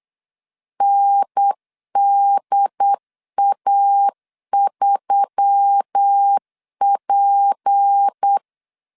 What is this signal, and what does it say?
Morse code, heard as a keyed tone, about 8 wpm, 800 Hz. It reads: NDA3P
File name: CW / DET / 1 6 1500C S B 8wpm 5142